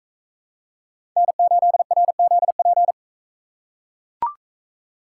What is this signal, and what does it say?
Morse code, heard as a keyed tone, about 40 wpm, 700 Hz. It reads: N8RZP